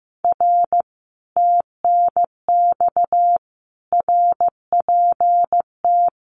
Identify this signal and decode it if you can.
Morse code, heard as a keyed tone, about 15 wpm, 700 Hz. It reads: R TNX RPT
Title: Common QSO Phrases Encore 15wpm